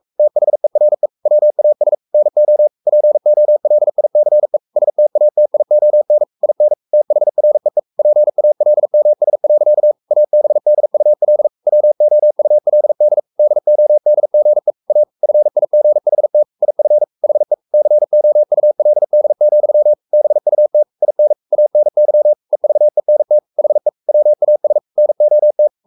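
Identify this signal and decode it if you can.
Morse code, heard as a keyed tone, about 44 wpm, 600 Hz. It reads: THERE WAS NO POLICE STATION IN THREE PALMS. ABDUL WOULD DODGE A FIGHT IF HE COULD, BUT IN ANY EVENT HE WAS NOT
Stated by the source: Morse code at 43 wpm sending Ken Follett's novel Never in English, from CW med Gnister.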